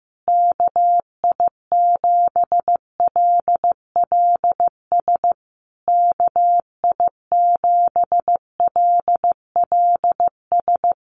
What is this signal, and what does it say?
Morse code, heard as a keyed tone, about 14 wpm, 700 Hz. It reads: KI7LLS KI7LLS
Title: USA Call Signs Repeated 3x 15wpm